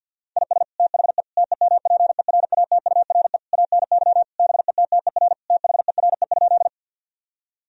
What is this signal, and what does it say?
Morse code, heard as a keyed tone, about 46 wpm, 700 Hz. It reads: IS THE TEMPERATURE ANY BETTER THERE?
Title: Sentences from Top 1000 Words - Review Part 4 - 50wpm